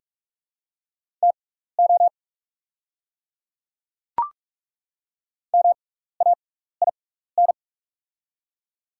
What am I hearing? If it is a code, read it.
Morse code, heard as a keyed tone, about 48 wpm, spaced as for 16 wpm, 690 Hz. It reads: TO MAIN
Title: 6x Character Spacing - Top 300 Words - 45wpm